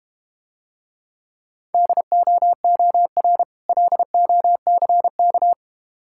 Morse code, heard as a keyed tone, about 31 wpm, 700 Hz. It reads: DOOR LOCK
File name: Easy Words - Speed-Racing with Hints - 25wpm